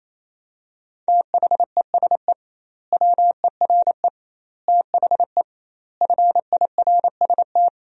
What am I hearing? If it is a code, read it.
Morse code, heard as a keyed tone, about 27 wpm, 700 Hz. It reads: THESE WERE THE FIRST
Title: 2x Word-Spacing - Sentences from Top 300 Words - 28wpm